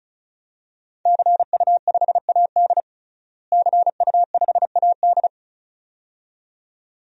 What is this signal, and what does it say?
Morse code, heard as a keyed tone, about 36 wpm, 700 Hz. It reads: CU5AD CU5AD
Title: DXCC-100 Call Signs - 5 Characters - Repeated 2x - 35wpm